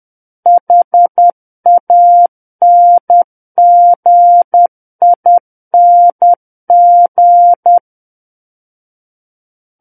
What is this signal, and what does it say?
Morse code, heard as a keyed tone, about 10 wpm, 700 Hz. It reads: HANGING
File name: Book One / 01-Chapter 1 - 10 WPM